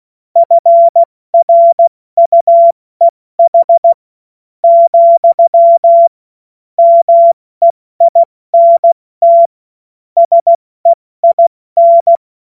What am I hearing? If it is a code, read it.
Morse code, heard as a keyed tone, about 16 wpm, spaced as for 14 wpm, 680 Hz. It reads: FRUEH , MEINT SEIN
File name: Morsekurs / Witze / Witz 13 Februar 2024 70 F80